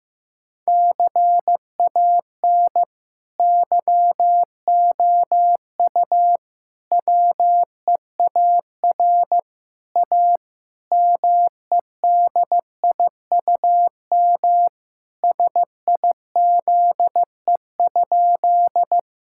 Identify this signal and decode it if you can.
Morse code, heard as a keyed tone, about 15 wpm, 700 Hz. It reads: CAN YOU WEAR A MEDIUM SIZE?
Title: Sentences from Top 1000 Words - Review Part 6 - 15wpm